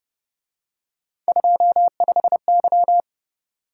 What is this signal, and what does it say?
Morse code, heard as a keyed tone, about 31 wpm, 700 Hz. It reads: 25Y
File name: Speed-Racing - Three Letter-Number Combinations - 25wpm